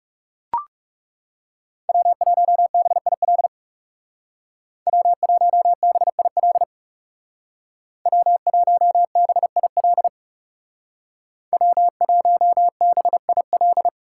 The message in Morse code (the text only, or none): W1BIL W1BIL W1BIL W1BIL